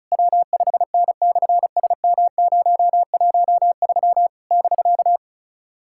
Morse code, heard as a keyed tone, about 36 wpm, 700 Hz. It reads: W5N/SM013 <BK>
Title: SOTA QSO Practice 35wpm